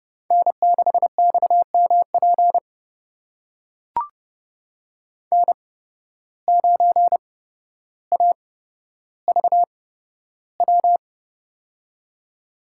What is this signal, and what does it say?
Morse code, heard as a keyed tone, about 30 wpm, spaced as for 13 wpm, 700 Hz. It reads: N6XMP N9AVW
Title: USA Call Signs - 5 Characters - 8x Character Spacing - 30wpm